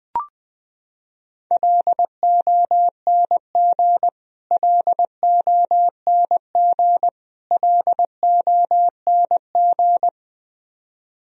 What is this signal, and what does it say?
Morse code, heard as a keyed tone, about 20 wpm, 700 Hz. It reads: LONG LONG LONG
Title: Top 100 Words Repeated 20wpm